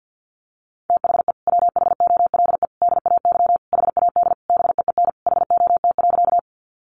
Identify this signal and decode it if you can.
Morse code, heard as a keyed tone, about 50 wpm, 700 Hz. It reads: THE WHOLE DAY HAD BEEN HOT.
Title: Sentences from Top 500 Words 50wpm